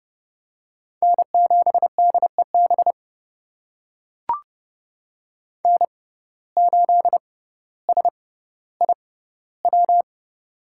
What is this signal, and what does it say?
Morse code, heard as a keyed tone, about 30 wpm, spaced as for 15 wpm, 700 Hz. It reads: N7DEB N8SIW
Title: USA Call Signs - 5 Characters - 6x Character Spacing - 30wpm